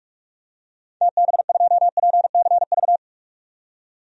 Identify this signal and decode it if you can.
Morse code, heard as a keyed tone, about 46 wpm, 690 Hz. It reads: TB2PCV
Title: DXCC-100 Call Signs - 6 Characters - 45wpm